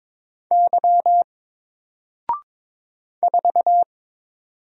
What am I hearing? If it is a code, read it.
Morse code, heard as a keyed tone, about 22 wpm, 700 Hz. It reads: Y 4
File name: Single-Letter-Number-22wpm